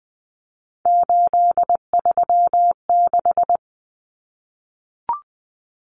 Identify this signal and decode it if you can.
Morse code, heard as a keyed tone, about 20 wpm, 700 Hz. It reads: 836
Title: Common Cut Numbers - 3-Digit 20wpm